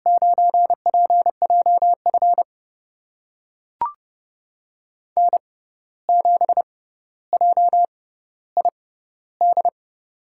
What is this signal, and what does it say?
Morse code, heard as a keyed tone, about 31 wpm, spaced as for 14 wpm, 700 Hz. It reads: 9PJF N7JID
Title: USA Call Signs - 5 Characters - 6x Character Spacing - 30wpm